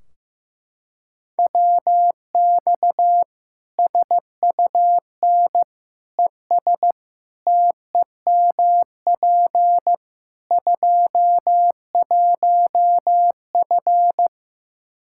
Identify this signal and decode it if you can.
Morse code, heard as a keyed tone, about 15 wpm, 700 Hz. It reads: WX SUN ES TEMP 21F
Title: Practice QSO - Exchange 2 - Info - 15wpm